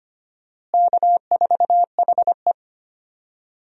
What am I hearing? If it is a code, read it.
Morse code, heard as a keyed tone, about 25 wpm, 700 Hz. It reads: K4HE
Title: USA Call Signs - 4 Characters - Repeated 2x - 25wpm